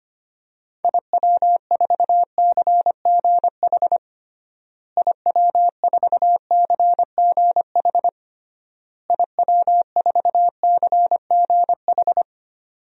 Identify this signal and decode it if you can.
Morse code, heard as a keyed tone, about 25 wpm, 700 Hz. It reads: IW4CGH IW4CGH IW4CGH
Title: DXCC-100 Call Signs - 6 Characters - Repeated 3x - 25wpm